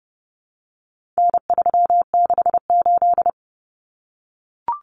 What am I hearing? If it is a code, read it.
Morse code, heard as a keyed tone, about 30 wpm, 700 Hz. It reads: N368
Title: Common Cut Numbers - 4-Digit 30wpm